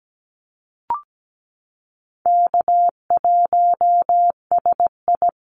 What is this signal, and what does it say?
Morse code, heard as a keyed tone, about 17 wpm, 700 Hz. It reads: K1SI
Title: USA Call Signs Repeated 3x 17wpm